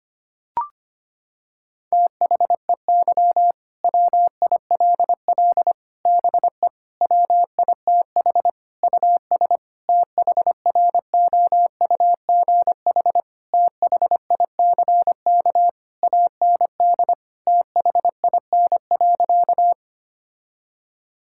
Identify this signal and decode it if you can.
Morse code, heard as a keyed tone, about 25 wpm, 700 Hz. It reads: THEY WILL BE WITH US THROUGH THICK AND THIN.
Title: Sentences from Top 1600 Words 25wpm